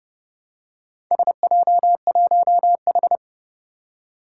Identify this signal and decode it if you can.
Morse code, heard as a keyed tone, about 30 wpm, 700 Hz. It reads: SJ1H
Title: Four Challenging Letter-Number Combinations 30wpm